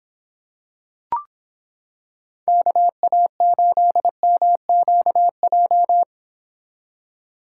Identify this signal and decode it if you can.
Morse code, heard as a keyed tone, about 26 wpm, 700 Hz. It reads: KA8MQJ